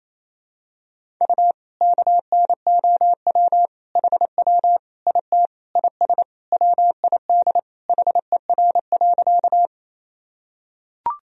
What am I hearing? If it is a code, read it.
Morse code, heard as a keyed tone, about 28 wpm, 700 Hz. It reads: U KNOW HW IT IS WID HER.
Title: Sentences from Top 100 Words and Abbreviations 28wpm